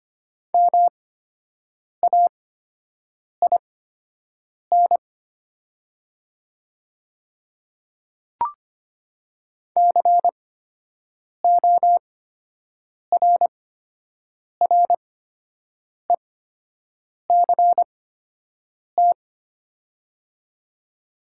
Morse code, heard as a keyed tone, about 25 wpm, spaced as for 7 wpm, 700 Hz. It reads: MAIN CORRECT
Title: 8x Character Spacing - Top 500 Words - 25wpm